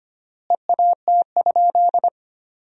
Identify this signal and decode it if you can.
Morse code, heard as a keyed tone, about 25 wpm, 700 Hz. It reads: EAT?
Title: 3x Word-Spacing - Sentences from Top 500 Words - 25wpm